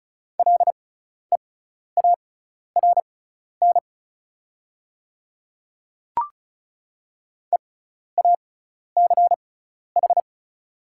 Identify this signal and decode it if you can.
Morse code, heard as a keyed tone, about 36 wpm, spaced as for 12 wpm, 700 Hz. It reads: LEARN EACH
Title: 6x Character Spacing - Top 300 Words - 35wpm